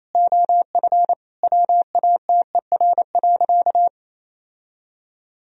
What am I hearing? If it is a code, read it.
Morse code, heard as a keyed tone, about 28 wpm, 700 Hz. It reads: OF WATER.
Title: Sentences from Top 1000 Words - Review Part 1 - 28wpm